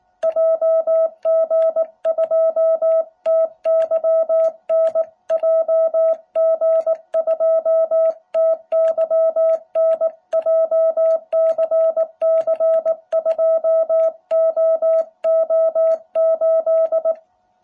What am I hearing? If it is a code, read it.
Morse code, heard as a keyed tone, about 19 wpm, 650 Hz. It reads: JG2TYN JG2TYN JCC2OO8